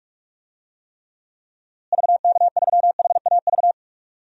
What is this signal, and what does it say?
Morse code, heard as a keyed tone, about 45 wpm, 700 Hz. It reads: VK3HAV
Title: DXCC-100 Call Signs - 6 Characters - Repeated 2x - 45wpm